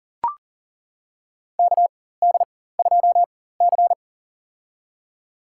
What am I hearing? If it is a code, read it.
Morse code, heard as a keyed tone, about 40 wpm, spaced as for 19 wpm, 700 Hz. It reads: KD2C
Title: USA Call Signs - 4 Characters - 4x Character Spacing - 40wpm